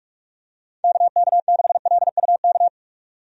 Morse code, heard as a keyed tone, about 46 wpm, 700 Hz. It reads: KK6LUK
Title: USA Call Signs - 6 Characters - 45wpm